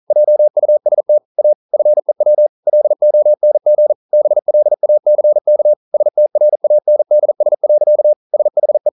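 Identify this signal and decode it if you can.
Morse code, heard as a keyed tone, about 41 wpm, 600 Hz. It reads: JUST A FEW LONG BLACK STRANDS. SHE
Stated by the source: Morse code practice at 41 wpm, from CW med Gnister.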